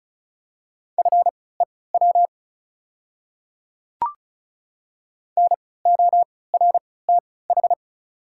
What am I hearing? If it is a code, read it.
Morse code, heard as a keyed tone, about 34 wpm, spaced as for 20 wpm, 700 Hz. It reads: FEW NORTH
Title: 3x Character Spacing - Top 300 Words - 35wpm